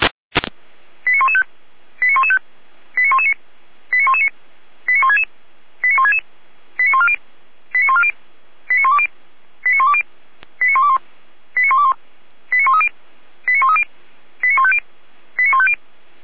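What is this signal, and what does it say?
Five-tone selective call (ZVEI), heard as quick runs of five tones, tones of about 70 ms. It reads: 89115 89115 89119 89119 89166 89166 89140 89140 89120 89120 89121 89121 89130 89130 89150 89150
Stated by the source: ZVEI five-tone call received on an SDR, recorded by the Feuerwehr Orlamünde fire brigade.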